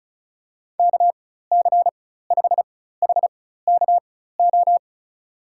Morse code, heard as a keyed tone, about 34 wpm, spaced as for 16 wpm, 700 Hz. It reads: KC5HKO
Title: USA Call Signs - 6 Characters - 4x Character Spacing - 35wpm